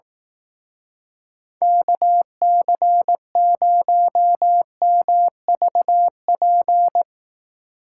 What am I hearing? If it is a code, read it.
Morse code, heard as a keyed tone, about 18 wpm, 700 Hz. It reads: KC0MVP
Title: Speed Racing - USA Call Signs 15wpm